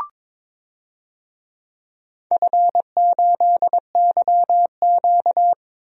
Morse code, heard as a keyed tone, about 22 wpm, 700 Hz. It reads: F8YQ
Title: DXCC-100 Call Signs - 4 Characters - 22wpm